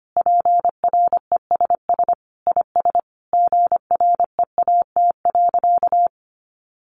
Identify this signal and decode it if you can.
Morse code, heard as a keyed tone, about 25 wpm, 700 Hz. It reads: PRESS IS GREAT.
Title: Sentences from Top 800 Words 25wpm